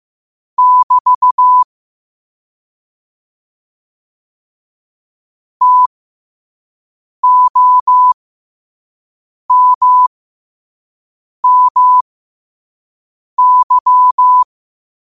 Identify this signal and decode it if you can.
Morse code, heard as a keyed tone, about 15 wpm, spaced as for 5 wpm, 1000 Hz. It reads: <BT> TOMMY
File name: qotdfiles / 5wpm